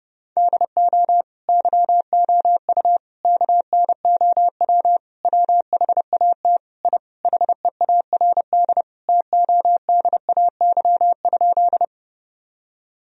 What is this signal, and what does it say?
Morse code, heard as a keyed tone, about 30 wpm, 700 Hz. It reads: DO YOU KNOW WHAT I HEARD TODAY?
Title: Sentences from Top 1000 Words - Review Part 4 - 30wpm